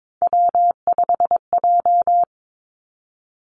Morse code, heard as a keyed tone, about 22 wpm, 700 Hz. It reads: W5J
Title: Three Challenging Letter-Number Combinations 22wpm